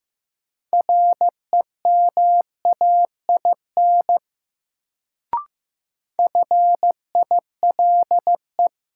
Morse code, heard as a keyed tone, about 16 wpm, 700 Hz. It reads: REMAIN FILE
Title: Sentences from Top 1200 Words - New Word Drill Alt - 15wpm